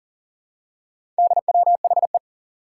Morse code, heard as a keyed tone, about 39 wpm, 700 Hz. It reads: DWHE
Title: Four Challenging Letter-Number Combinations 40wpm